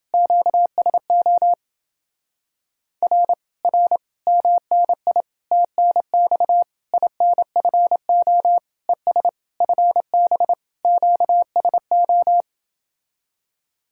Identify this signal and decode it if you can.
Morse code, heard as a keyed tone, about 30 wpm, 700 Hz. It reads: QSO R R MNI TNX INFO ES FB QSO